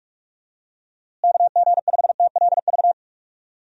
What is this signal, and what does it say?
Morse code, heard as a keyed tone, about 46 wpm, 700 Hz. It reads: KG5TLV
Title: USA Call Signs - 6 Characters - Repeated 3x - 45wpm